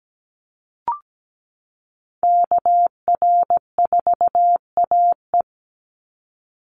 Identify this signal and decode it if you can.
Morse code, heard as a keyed tone, about 17 wpm, 700 Hz. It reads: KR4AE
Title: Call Signs Repeated 2x - 17wpm